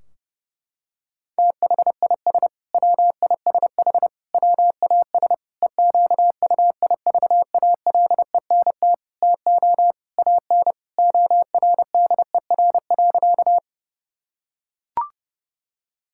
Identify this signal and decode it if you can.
Morse code, heard as a keyed tone, about 32 wpm, 700 Hz. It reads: THIS WISH WAS EQUIVALENT TO AN ORDER.
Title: Sentences from Top 1600 Words 30wpm